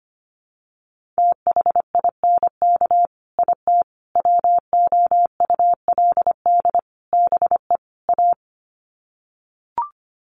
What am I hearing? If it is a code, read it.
Morse code, heard as a keyed tone, about 25 wpm, 700 Hz. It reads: THINK IT WOULD BE A
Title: Sets of 5 Words Encore2 25wpm